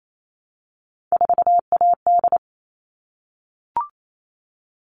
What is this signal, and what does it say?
Morse code, heard as a keyed tone, about 28 wpm, 700 Hz. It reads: 4AD
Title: All Cut Numbers - 3-Digit 28wpm